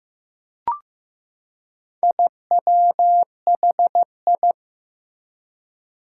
Morse code, heard as a keyed tone, about 15 wpm, 700 Hz. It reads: IWHI
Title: Four Challenging Letter-Number Combinations 15wpm